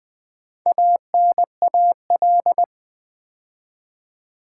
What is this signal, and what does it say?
Morse code, heard as a keyed tone, about 20 wpm, 700 Hz. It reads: ANAL
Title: Build A Word - 4 Letters At A Time - 8-Letter Words - 20wpm